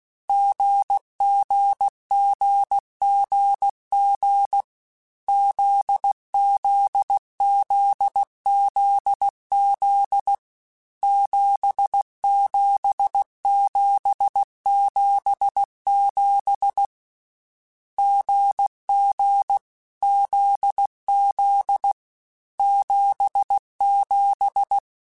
Morse code, heard as a keyed tone, about 16 wpm, 800 Hz. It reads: GGGGG ZZZZZ 77777 GG ZZ 77
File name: Short Runs / Short Runs-mp3s / GZ GZ7 @15